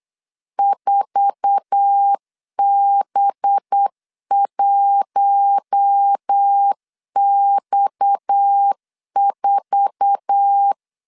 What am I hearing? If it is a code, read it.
Morse code, heard as a keyed tone, about 8 wpm, 800 Hz. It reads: 4B1X4
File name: CW / DET / 1 9 1500C S C 8wpm 5653